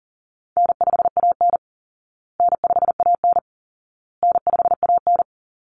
Morse code, heard as a keyed tone, about 40 wpm, 700 Hz. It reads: N5AN N5AN N5AN
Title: Easy Call Signs Repeated 3x - 40wpm